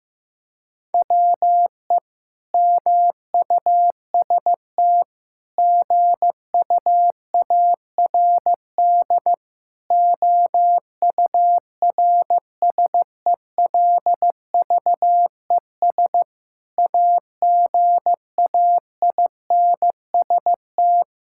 Morse code, heard as a keyed tone, about 15 wpm, 700 Hz. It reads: WE MUST GUARD OURSELVES AGAINST
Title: Sentences from Top 1600 Words 15wpm